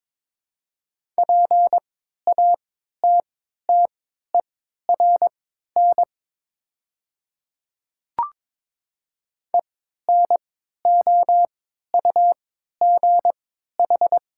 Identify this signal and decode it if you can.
Morse code, heard as a keyed tone, about 22 wpm, spaced as for 13 wpm, 700 Hz. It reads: PATTERN ENOUGH
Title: 3x Character Spacing - Top 500 Words - 22wpm